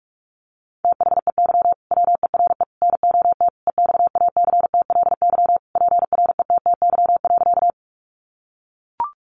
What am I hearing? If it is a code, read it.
Morse code, heard as a keyed tone, about 44 wpm, 700 Hz. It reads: THEY WERE NOT EXACTLY PRETTY.
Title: Sentences from Top 1000 Words 45wpm